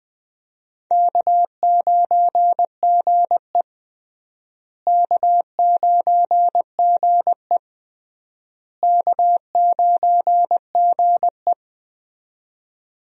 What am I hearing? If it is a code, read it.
Morse code, heard as a keyed tone, about 20 wpm, 700 Hz. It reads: K9GE K9GE K9GE
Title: USA Call Signs - 4 Characters - Repeated 3x - 20wpm